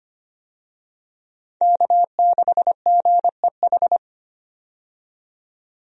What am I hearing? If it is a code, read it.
Morse code, heard as a keyed tone, about 25 wpm, 700 Hz. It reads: K6GEH
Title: USA Call Signs - 5 Characters - 25wpm